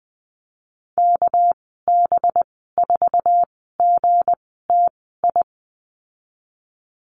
Morse code, heard as a keyed tone, about 20 wpm, spaced as for 15 wpm, 700 Hz. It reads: KB4GTI
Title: 200 Lesson 32g - Letter k - Call Signs 20wpm